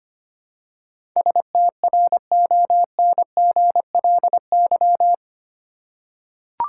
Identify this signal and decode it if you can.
Morse code, heard as a keyed tone, about 25 wpm, 700 Hz. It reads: STRONGLY
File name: 8 Letter Words - First 3 Letters Given - 25wpm